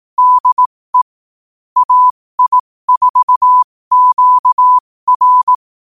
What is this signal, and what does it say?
Morse code, heard as a keyed tone, about 18 wpm, spaced as for 15 wpm, 1000 Hz. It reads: DE AI4QR
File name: qotdfiles / 15wpm